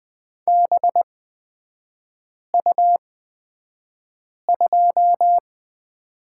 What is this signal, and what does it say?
Morse code, heard as a keyed tone, about 20 wpm, spaced as for 5 wpm, 700 Hz. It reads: BU2
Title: 224 Lesson 36e - BT - Three Characters 20wpm